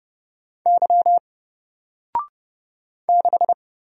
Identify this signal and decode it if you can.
Morse code, heard as a keyed tone, about 30 wpm, 700 Hz. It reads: Y 6
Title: Single Letter-Number Alt 30wpm